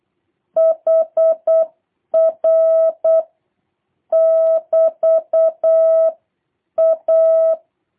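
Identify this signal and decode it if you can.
Morse code, heard as a keyed tone, about 8 wpm, 650 Hz. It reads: HR <BT> A
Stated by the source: Morse copy drill by JH8CBH: 24 random characters, A, E, N, T six times each, 3-second gaps for seven questions, 2-second gaps for three.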